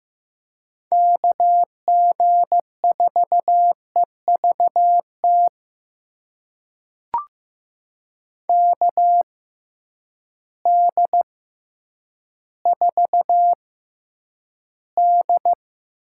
Morse code, heard as a keyed tone, about 15 wpm, spaced as for 9 wpm, 700 Hz. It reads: KG4EVT KD4D
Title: USA Call Signs - 6x Character Spacing - 15wpm